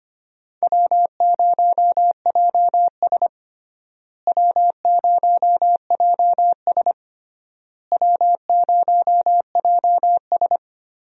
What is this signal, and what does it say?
Morse code, heard as a keyed tone, about 25 wpm, 700 Hz. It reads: W0JS W0JS W0JS
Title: USA Call Signs - 4 Characters - Repeated 3x - 25wpm